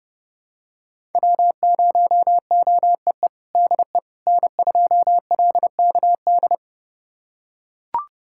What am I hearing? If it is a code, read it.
Morse code, heard as a keyed tone, about 30 wpm, 700 Hz. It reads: W0OEE DE N2LKD
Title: USA Callsign DE Callsign - 5 Characters - 30wpm